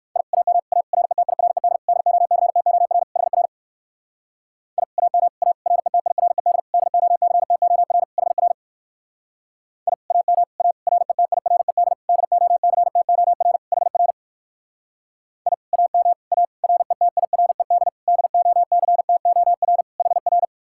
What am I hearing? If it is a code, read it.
Morse code, heard as a keyed tone, about 59 wpm, 700 Hz. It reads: I AM A RETIRED DOCTOR HR I AM A RETIRED DOCTOR HR I AM A RETIRED DOCTOR HR I AM A RETIRED DOCTOR HR